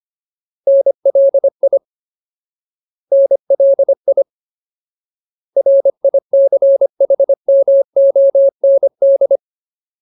Morse code, heard as a keyed tone, about 25 wpm, 550 Hz. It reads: NLI NLI RICHMOND
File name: Ultimate Word List / 25WPM Ultimate